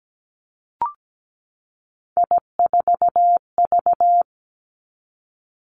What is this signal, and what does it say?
Morse code, heard as a keyed tone, about 17 wpm, 700 Hz. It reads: I4V
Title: Three Challenging Letter-Number Combinations 17wpm